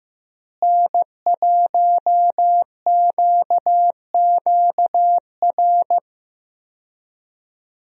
Morse code, heard as a keyed tone, about 15 wpm, 700 Hz. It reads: N1QQR
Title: USA Call Signs - 5 Characters - Repeated 2x - 15wpm